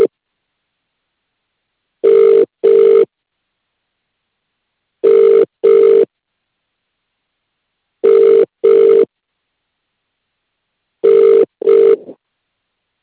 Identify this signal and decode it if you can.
Ringback tone (UK), 400 + 450 Hz.